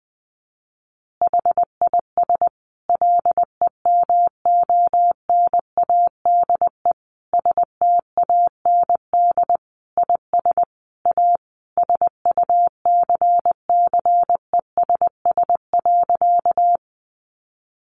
Morse code, heard as a keyed tone, about 19 wpm, 700 Hz. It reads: HIS LEMONADE STAND IS A SUCCESS.
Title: Additional Easy Phrases 20wpm